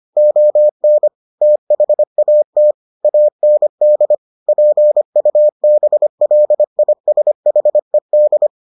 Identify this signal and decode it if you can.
Morse code, heard as a keyed tone, about 25 wpm, 600 Hz. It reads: ON THAT AND PUBLISHED